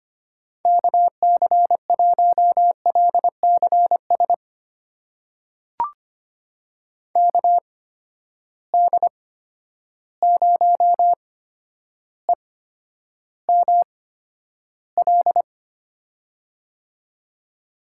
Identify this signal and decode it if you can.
Morse code, heard as a keyed tone, about 25 wpm, spaced as for 11 wpm, 700 Hz. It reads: KC1LCS KD0EML
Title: USA Call Signs - 6 Characters - 8x Character Spacing - 25wpm